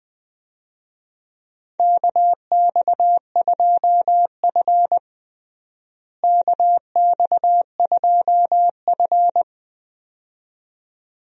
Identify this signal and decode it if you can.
Morse code, heard as a keyed tone, about 21 wpm, 700 Hz. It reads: KX2F KX2F
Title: USA Call Signs - 4 Characters - Repeated 2x - 20wpm